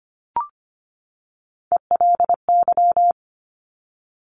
Morse code, heard as a keyed tone, about 25 wpm, 700 Hz. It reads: ELY
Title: Top Three Letter Combinations in Words 25wpm